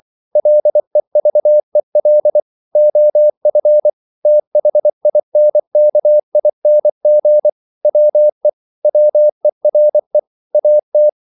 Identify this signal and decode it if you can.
Morse code, heard as a keyed tone, about 24 wpm, 600 Hz. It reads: LEVEL OF THINKING WE WERE AT
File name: CW / Practice / EnglishQuotations0224